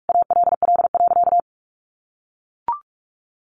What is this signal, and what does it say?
Morse code, heard as a keyed tone, about 44 wpm, 700 Hz. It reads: ALL.
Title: Sentences from Top 1400 Words 45wpm